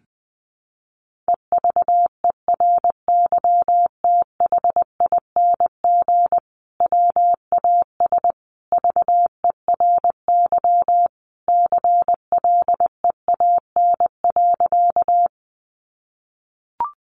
Morse code, heard as a keyed tone, about 20 wpm, 700 Hz. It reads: EVERYTHING WAS VERY CLEAN.
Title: Sentences from Top 1100 Words 20wpm